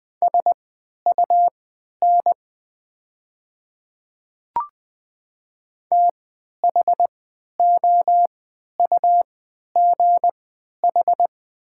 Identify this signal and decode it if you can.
Morse code, heard as a keyed tone, about 20 wpm, spaced as for 11 wpm, 700 Hz. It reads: SUN THOUGH
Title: 3x Character Spacing - Top 500 Words - 20wpm